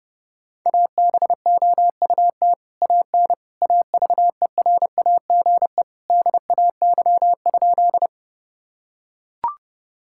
Morse code, heard as a keyed tone, about 30 wpm, 700 Hz. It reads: ABOUT AN AVERAGE DAY?
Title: Sentences from Top 1000 Words - Review Part 6 - 30wpm